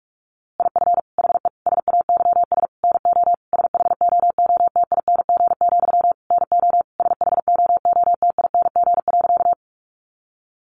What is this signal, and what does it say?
Morse code, heard as a keyed tone, about 44 wpm, 700 Hz. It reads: IF HE SAYS NO SHOOTING, NO SHOOTING.
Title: A Sound Of Thunder 45wpm